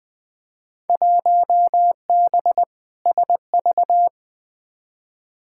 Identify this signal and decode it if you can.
Morse code, heard as a keyed tone, about 20 wpm, 700 Hz. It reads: 1B SV
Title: Field Day Exchanges - 20wpm